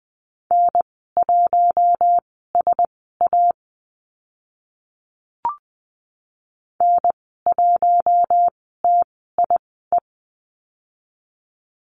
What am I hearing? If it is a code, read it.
Morse code, heard as a keyed tone, about 20 wpm, spaced as for 15 wpm, 700 Hz. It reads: N1SA N1TIE
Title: 042 Lesson 8g - Number 1 - Call Signs 20wpm